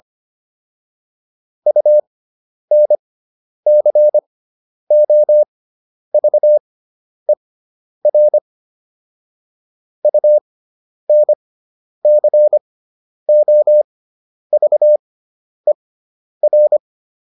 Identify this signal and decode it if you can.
Morse code, heard as a keyed tone, about 25 wpm, spaced as for 10 wpm, 600 Hz. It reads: UNCOVER UNCOVER